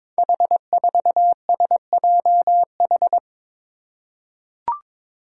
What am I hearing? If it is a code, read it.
Morse code, heard as a keyed tone, about 22 wpm, 700 Hz. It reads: H4SJH
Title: Five Challenging Letter-Number Combinations 22wpm